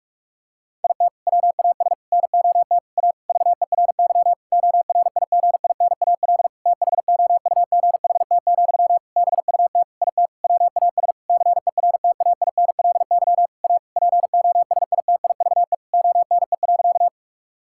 Morse code, heard as a keyed tone, about 46 wpm, 700 Hz. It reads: IT WAS NOT A VERY ORIGINAL THOUGHT, BUT IT WAS CERTAINLY A POSITIVE ONE.
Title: Sentences from Top 1000 Words - Review Part 6 - 45wpm